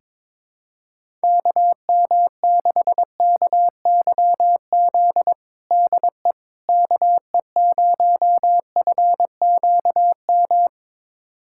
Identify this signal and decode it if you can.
Morse code, heard as a keyed tone, about 21 wpm, 700 Hz. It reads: KM6KYZ DE KE0FQM
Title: USA Callsign DE Callsign 22wpm